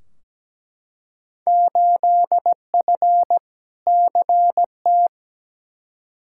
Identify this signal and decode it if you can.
Morse code, heard as a keyed tone, about 17 wpm, 700 Hz. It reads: 8F CT
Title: Field Day Exchanges Encore - 17wpm